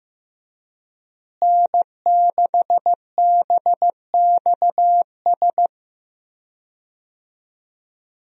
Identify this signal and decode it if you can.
Morse code, heard as a keyed tone, about 15 wpm, 700 Hz. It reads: N6BXS
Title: USA Call Signs - 5 Characters - Repeated 3x - 15wpm